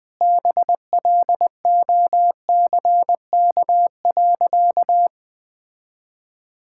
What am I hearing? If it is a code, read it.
Morse code, heard as a keyed tone, about 20 wpm, 700 Hz. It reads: BLOCK.
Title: Sentences from Top 1600 Words 20wpm